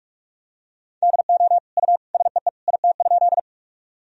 Morse code, heard as a keyed tone, about 46 wpm, 700 Hz. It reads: DO U SEE IT?
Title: Sentences from Top 100 Words and Abbreviations 45wpm